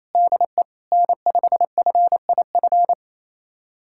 Morse code, heard as a keyed tone, about 27 wpm, 700 Hz. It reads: DE N5FIF
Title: USA Callsign DE Callsign 28wpm